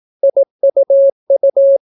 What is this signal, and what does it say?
Morse code, heard as a keyed tone, about 18 wpm, 550 Hz. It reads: IUU